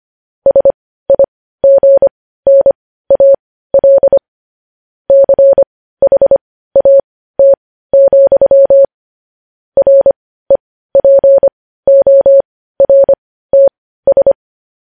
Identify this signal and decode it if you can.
Morse code, heard as a keyed tone, about 25 wpm, spaced as for 15 wpm, 550 Hz. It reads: SIGNAL CHAT, REPORTS
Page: News Headlines in Morse Code at 15 WPM